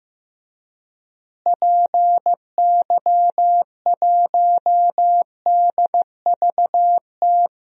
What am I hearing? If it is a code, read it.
Morse code, heard as a keyed tone, about 15 wpm, 700 Hz. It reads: PY1DVT
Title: DXCC-100 Call Signs - 6 Characters - Repeated 3x - 15wpm